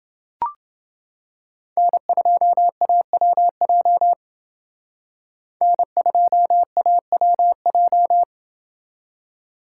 Morse code, heard as a keyed tone, about 28 wpm, 700 Hz. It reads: N2AWJ N2AWJ